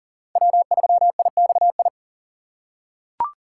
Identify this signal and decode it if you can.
Morse code, heard as a keyed tone, about 38 wpm, 700 Hz. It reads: W3IXI
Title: USA Call Signs - 5 Characters - 2x Character Spacing - 40wpm